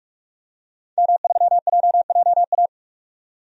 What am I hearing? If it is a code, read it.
Morse code, heard as a keyed tone, about 45 wpm, 700 Hz. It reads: M3JJA
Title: DXCC-100 Call Signs - 5 Characters - 45wpm